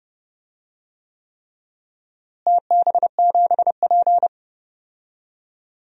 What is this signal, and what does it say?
Morse code, heard as a keyed tone, about 29 wpm, 700 Hz. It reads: TB7P
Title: DXCC-100 Call Signs - 4 Characters - Repeated 3x - 30wpm